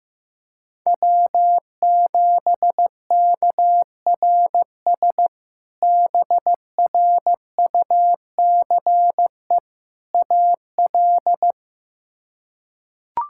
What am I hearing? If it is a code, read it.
Morse code, heard as a keyed tone, about 15 wpm, 700 Hz. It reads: W7KRS BRUCE AL
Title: K1USN SST QSO Practice - 15wpm